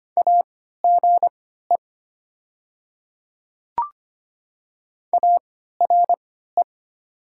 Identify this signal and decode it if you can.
Morse code, heard as a keyed tone, about 25 wpm, spaced as for 14 wpm, 700 Hz. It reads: AGE ARE
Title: 3x Character Spacing - Top 500 Words - 25wpm